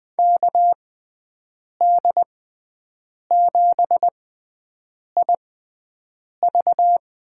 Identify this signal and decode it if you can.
Morse code, heard as a keyed tone, about 20 wpm, spaced as for 7 wpm, 700 Hz. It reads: KD7IV
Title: USA Call Signs - 6x Character Spacing - 20wpm